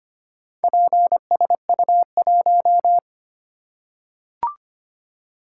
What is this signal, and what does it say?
Morse code, heard as a keyed tone, about 25 wpm, 700 Hz. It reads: PSU1
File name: practice / 25wpm / Four Letter-Number Combinations - 25wpm